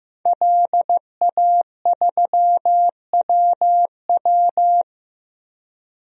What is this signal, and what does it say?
Morse code, heard as a keyed tone, about 15 wpm, 700 Hz. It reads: LA3WW
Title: DXCC-100 Call Signs - 5 Characters - Repeated 2x - 15wpm